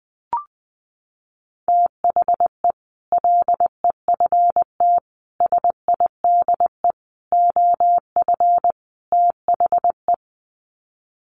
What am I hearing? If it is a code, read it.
Morse code, heard as a keyed tone, about 20 wpm, 700 Hz. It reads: THE LEFT SIDE OF THE
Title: Sets of 5 Words Encore 20wpm